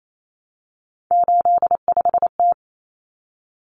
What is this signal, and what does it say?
Morse code, heard as a keyed tone, about 28 wpm, 700 Hz. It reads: 85T
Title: Common Cut Numbers - 3-Digit 28wpm